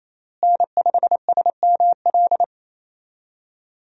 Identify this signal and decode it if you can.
Morse code, heard as a keyed tone, about 28 wpm, 700 Hz. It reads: N5SML
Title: USA Call Signs - 5 Characters - Repeated 2x - 28wpm